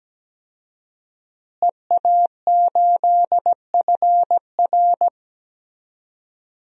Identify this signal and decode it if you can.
Morse code, heard as a keyed tone, about 17 wpm, 700 Hz. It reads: EA8FR
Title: DXCC-100 Call Signs - 5 Characters - Repeated 3x - 17wpm